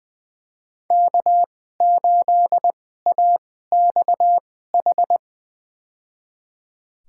Morse code, heard as a keyed tone, about 20 wpm, spaced as for 14 wpm, 700 Hz. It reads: K8AXH
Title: USA Call Signs - 5 Characters - 2x Character Spacing - 20wpm